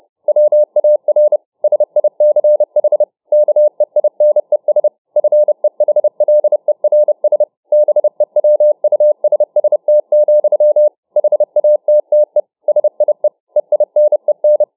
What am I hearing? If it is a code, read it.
Morse code, heard as a keyed tone, about 29 wpm, 600 Hz. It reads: WAR SICH KEINES FEHLERS BEWUSST, HATTE SIE EINEN